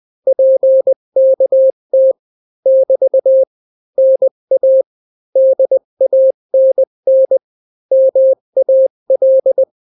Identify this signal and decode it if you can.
Morse code, heard as a keyed tone, about 20 wpm, 530 Hz. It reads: PKT <BT> NA DANN MAL
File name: daily / 18wpm